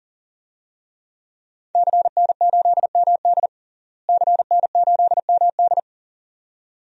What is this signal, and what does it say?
Morse code, heard as a keyed tone, about 37 wpm, 700 Hz. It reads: CN8MD CN8MD
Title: DXCC-100 Call Signs - 5 Characters - Repeated 2x - 40wpm